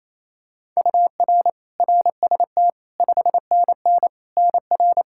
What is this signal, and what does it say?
Morse code, heard as a keyed tone, about 28 wpm, 700 Hz. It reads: UR RST 5NN NR
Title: Practice QSO - Exchange 1 - Beginning - 28wpm